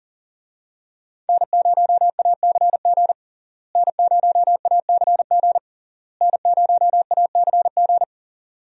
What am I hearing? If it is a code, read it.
Morse code, heard as a keyed tone, about 40 wpm, 700 Hz. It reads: N0ACG N0ACG N0ACG
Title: USA Call Signs - 5 Characters - Repeated 3x - 40wpm